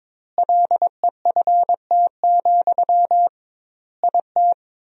Morse code, heard as a keyed tone, about 22 wpm, 700 Hz. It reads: LEFT, IT
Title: 2x Word-Spacing - Sentences from Top 400 Words - 22wpm